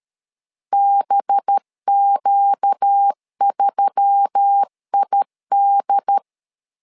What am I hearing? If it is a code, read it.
Morse code, heard as a keyed tone, about 13 wpm, 800 Hz. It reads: BQ3ID
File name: LEZIONI / 5 LBVC / 5 LB 12wpm 1906